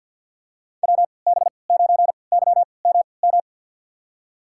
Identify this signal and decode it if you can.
Morse code, heard as a keyed tone, about 46 wpm, spaced as for 27 wpm, 700 Hz. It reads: WB9YMM
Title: USA Call Signs - 6 Characters - 3x Character Spacing - 50wpm